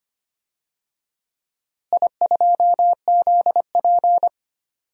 Morse code, heard as a keyed tone, about 25 wpm, 700 Hz. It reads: I2ZP
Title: DXCC-100 Call Signs - 4 Characters - Repeated 3x - 25wpm